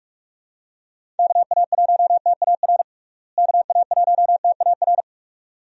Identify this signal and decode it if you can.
Morse code, heard as a keyed tone, about 46 wpm, 700 Hz. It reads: KA1TAR KA1TAR
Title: USA Call Signs - 6 Characters - Repeated 2x - 45wpm